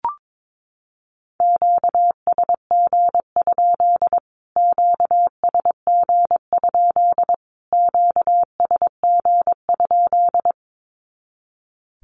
Morse code, heard as a keyed tone, about 22 wpm, 700 Hz. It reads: QSG? QSG? QSG?
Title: Q-Codes Repeated 3x 22wpm